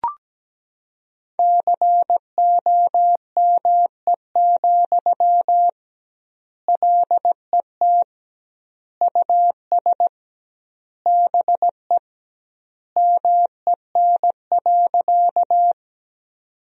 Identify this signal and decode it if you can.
Morse code, heard as a keyed tone, about 17 wpm, 700 Hz. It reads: COME, LET US BE MEN.
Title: 2x Word-Spacing - Sentences from Top 300 Words - 17wpm